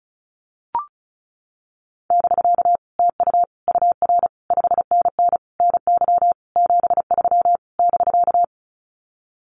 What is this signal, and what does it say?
Morse code, heard as a keyed tone, about 34 wpm, 700 Hz. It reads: <BK> TU UR 5NN NY 73 <BK>
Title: Common QSO Phrases Encore 35wpm